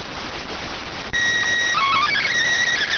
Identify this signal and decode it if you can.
SSTV (slow-scan television), PD 290.